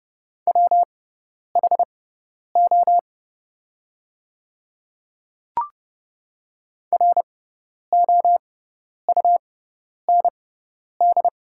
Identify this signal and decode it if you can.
Morse code, heard as a keyed tone, about 30 wpm, spaced as for 10 wpm, 700 Hz. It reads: WHO ROUND
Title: 6x Character Spacing - Top 200 Words - 30wpm